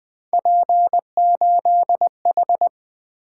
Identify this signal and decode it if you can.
Morse code, heard as a keyed tone, about 20 wpm, 700 Hz. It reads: P8H
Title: Three Letter-Number Combinations - 20wpm